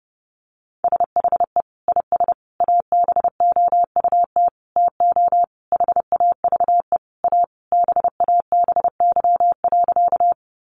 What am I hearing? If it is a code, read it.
Morse code, heard as a keyed tone, about 28 wpm, 700 Hz. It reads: SHE IS ABOUT TO HAVE A BABY.
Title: Sentences from Top 1300 Words 30wpm